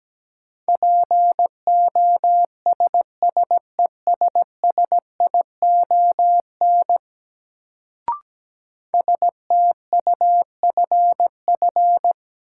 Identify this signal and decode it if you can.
Morse code, heard as a keyed tone, about 17 wpm, 700 Hz. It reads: POSSESSION STUFF
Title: Sentences from Top 1400 Words - New Word Drill Alt - 17wpm